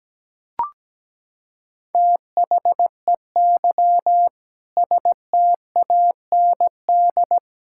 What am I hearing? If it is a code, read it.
Morse code, heard as a keyed tone, about 17 wpm, 700 Hz. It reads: THEY STAND
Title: Sentences from Top 1000 Words - Review Part 5 - 17wpm